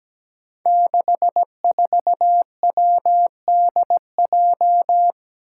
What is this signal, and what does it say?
Morse code, heard as a keyed tone, about 17 wpm, 700 Hz. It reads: 64WDJ
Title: Five Challenging Letter-Number Combinations 17wpm